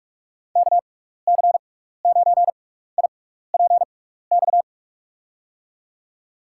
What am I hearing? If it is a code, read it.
Morse code, heard as a keyed tone, about 46 wpm, spaced as for 16 wpm, 700 Hz. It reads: KC9IPX
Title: USA Call Signs - 6x Character Spacing - 45wpm